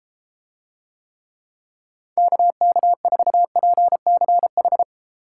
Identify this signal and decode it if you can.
Morse code, heard as a keyed tone, about 32 wpm, 700 Hz. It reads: KK4PCH